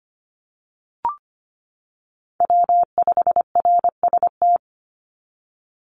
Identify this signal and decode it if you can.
Morse code, heard as a keyed tone, about 25 wpm, 700 Hz. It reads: W5RST
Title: NA SOTA Chaser Callsigns 25wpm